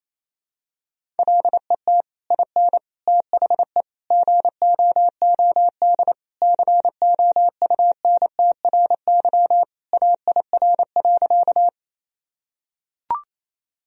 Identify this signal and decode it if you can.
Morse code, heard as a keyed tone, about 28 wpm, 700 Hz. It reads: LET IN THE GOOD COUNTRY AIR.
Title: Sentences from Top 1000 Words - Review Part 1 - 28wpm